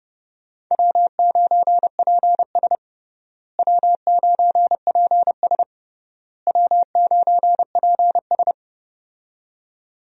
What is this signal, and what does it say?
Morse code, heard as a keyed tone, about 30 wpm, 700 Hz. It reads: W9PS W9PS W9PS
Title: USA Call Signs - 4 Characters - Repeated 3x - 30wpm